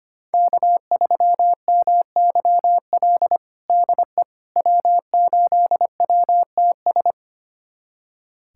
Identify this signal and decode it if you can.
Morse code, heard as a keyed tone, about 25 wpm, 700 Hz. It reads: K3MYL DE W8WTS
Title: USA Callsign DE Callsign - 5 Characters - 25wpm